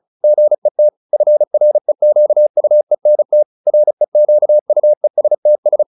Morse code, heard as a keyed tone, about 34 wpm, 600 Hz. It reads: GET FREQUENT REQUESTS